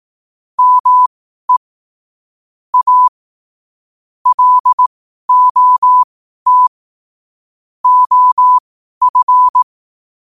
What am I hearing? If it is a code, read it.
Morse code, heard as a keyed tone, about 18 wpm, spaced as for 12 wpm, 1000 Hz. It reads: ME A LOT OF
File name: qotdfiles / 12wpm